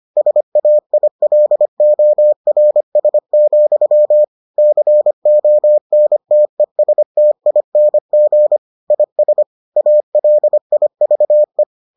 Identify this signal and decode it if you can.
Morse code, heard as a keyed tone, about 25 wpm, 600 Hz. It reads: SAILORS, CONTESTING IS ALIVE